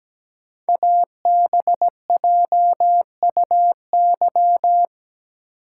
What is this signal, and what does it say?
Morse code, heard as a keyed tone, about 17 wpm, 700 Hz. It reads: ABJUY
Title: Code Groups - 5 Letters - 17wpm